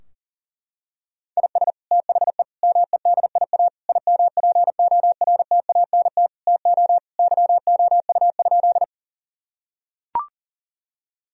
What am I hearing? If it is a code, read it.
Morse code, heard as a keyed tone, about 40 wpm, 700 Hz. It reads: IS THE MEDIA IMPORTANT TO YOU?
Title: Sentences from Top 1000 Words - Review Part 4 - 40wpm